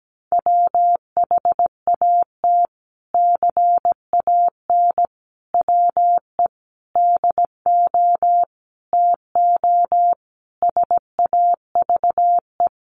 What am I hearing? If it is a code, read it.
Morse code, heard as a keyed tone, about 17 wpm, 700 Hz. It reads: WHAT CAN WE DO TO SAVE
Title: Sentences from Top 1100 Words 17wpm